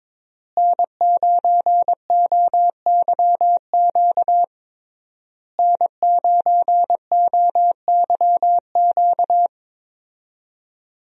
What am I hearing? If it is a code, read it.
Morse code, heard as a keyed tone, about 22 wpm, 700 Hz. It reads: N9OYQ N9OYQ
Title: USA Call Signs - 5 Characters - Repeated 3x - 22wpm